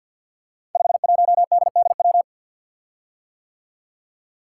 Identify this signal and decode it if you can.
Morse code, heard as a keyed tone, about 52 wpm, 700 Hz. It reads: 51DDW
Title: Code Groups - 5 Letter:Numbers - 50wpm